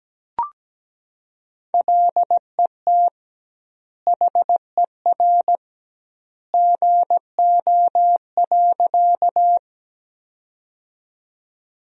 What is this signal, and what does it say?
Morse code, heard as a keyed tone, about 17 wpm, 700 Hz. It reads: LET HER GO.
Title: 2x Word-Spacing - Sentences from Top 300 Words - 17wpm